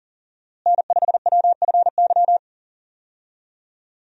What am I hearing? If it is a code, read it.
Morse code, heard as a keyed tone, about 40 wpm, 700 Hz. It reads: N5WFY
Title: USA Call Signs - 5 Characters - Repeated 3x - 40wpm